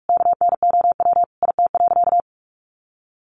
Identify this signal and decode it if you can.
Morse code, heard as a keyed tone, about 44 wpm, 700 Hz. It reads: KNOW IT.
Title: Sentences from Top 900 Words 45wpm